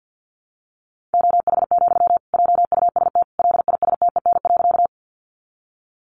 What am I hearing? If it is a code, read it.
Morse code, heard as a keyed tone, about 50 wpm, 700 Hz. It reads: OH, JUST LISTEN.
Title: Sentences from Top 500 Words 50wpm